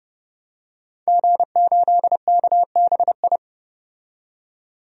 Morse code, heard as a keyed tone, about 32 wpm, 700 Hz. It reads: G8KBI
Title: DXCC-100 Call Signs - 5 Characters - 30wpm